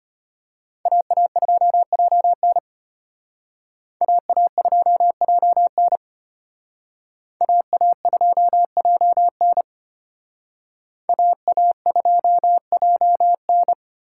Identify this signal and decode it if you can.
Morse code, heard as a keyed tone, about 32 wpm, 700 Hz. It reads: AA2JN AA2JN AA2JN AA2JN